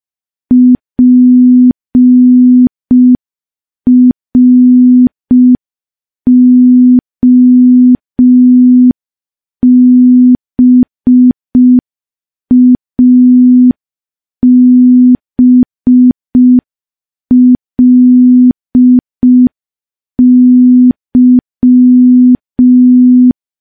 Morse code, heard as a keyed tone, about 5 wpm, 250 Hz. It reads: PROBABLY
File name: morse / morse-training-05WPM